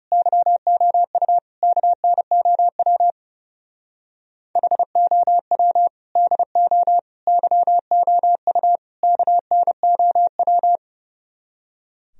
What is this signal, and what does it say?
Morse code, heard as a keyed tone, about 36 wpm, 700 Hz. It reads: YOU KNOW HOW DO YOU KNOW
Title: Speed-Racing - Sets of 4 Words 30wpm